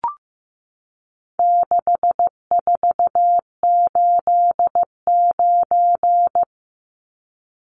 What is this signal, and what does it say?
Morse code, heard as a keyed tone, about 15 wpm, 700 Hz. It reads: 6489
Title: Numbers - 4-Digit 15wpm